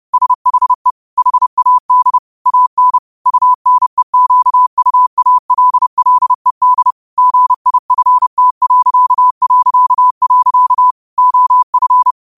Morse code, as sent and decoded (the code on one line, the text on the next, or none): ... .... / .... .- -.. / .- -. / ..- -. --.- ..- .- .-.. .-.. -.. / --. .. ..-. - .-.-.- .-.-.- .-.-.- / --- ..-.
SHE HAD AN UNEQUALLED GIFT... OF